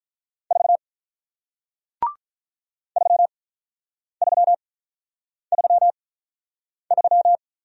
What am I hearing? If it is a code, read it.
Morse code, heard as a keyed tone, about 46 wpm, spaced as for 9 wpm, 700 Hz. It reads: 4 3333